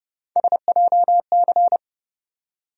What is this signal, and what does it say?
Morse code, heard as a keyed tone, about 30 wpm, 700 Hz. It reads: SJC
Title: USA Call Sign Postfixes - 3 Letters 30wpm